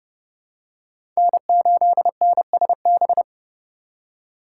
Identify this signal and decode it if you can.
Morse code, heard as a keyed tone, about 30 wpm, 700 Hz. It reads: N8NSB
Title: USA Call Signs - 5 Characters - 2x Character Spacing - 30wpm